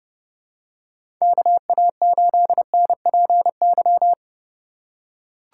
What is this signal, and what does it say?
Morse code, heard as a keyed tone, about 30 wpm, 700 Hz. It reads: KA8NPY
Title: USA Call Signs - 4x Character Spacing - 30wpm